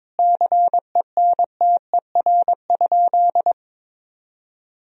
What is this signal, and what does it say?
Morse code, heard as a keyed tone, about 22 wpm, 700 Hz. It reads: CENTER?
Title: Sentences from Top 1000 Words - Review Part 2 - 22wpm